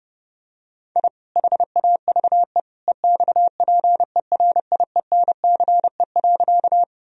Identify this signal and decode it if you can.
Morse code, heard as a keyed tone, about 32 wpm, 700 Hz. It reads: I HAVE EXPERIENCE.
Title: Sentences from Top 1000 Words - Review Part 3 - 30wpm